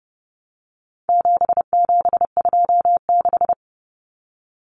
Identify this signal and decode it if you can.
Morse code, heard as a keyed tone, about 30 wpm, 700 Hz. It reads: 7726
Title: Numbers - 4-Digit 30wpm